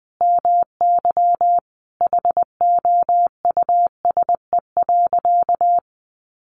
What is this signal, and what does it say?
Morse code, heard as a keyed tone, about 20 wpm, 700 Hz. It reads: MY HOUSE.
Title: Sentences from Top 1100 Words 20wpm